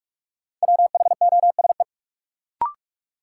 Morse code, heard as a keyed tone, about 46 wpm, 700 Hz. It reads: WHOSE
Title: Sentences from Top 1200 Words - New Word Drill Alt - 45wpm